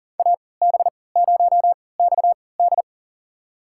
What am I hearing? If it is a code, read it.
Morse code, heard as a keyed tone, about 40 wpm, spaced as for 23 wpm, 700 Hz. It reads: AB0XD
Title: USA Call Signs - 5 Characters - 3x Character Spacing - 40wpm